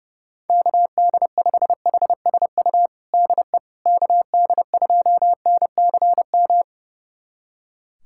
Morse code, heard as a keyed tone, about 30 wpm, 700 Hz. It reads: KD5HSU DE KD2NCM
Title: USA Callsign DE Callsign - 6 Characters - 30wpm